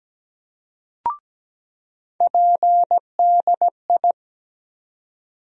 Morse code, heard as a keyed tone, about 17 wpm, 700 Hz. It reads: PDI
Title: USA Call Sign Postfixes - 3 Letters 17wpm